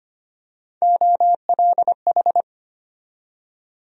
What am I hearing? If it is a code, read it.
Morse code, heard as a keyed tone, about 25 wpm, 700 Hz. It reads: OLH
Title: USA Call Sign Postfixes - 3 Letters 25wpm